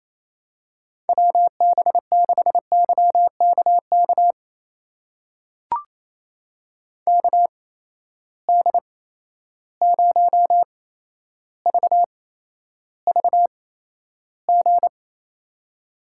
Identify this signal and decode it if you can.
Morse code, heard as a keyed tone, about 28 wpm, spaced as for 12 wpm, 700 Hz. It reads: WB6YKK KD0VVG
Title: USA Call Signs - 6 Characters - 8x Character Spacing - 28wpm